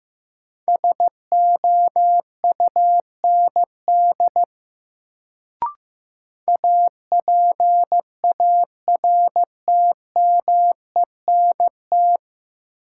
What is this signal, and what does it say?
Morse code, heard as a keyed tone, about 15 wpm, 700 Hz. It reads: SOUND APARTMENT
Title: Sentences from Top 900 Words - New Word Drill - 15wpm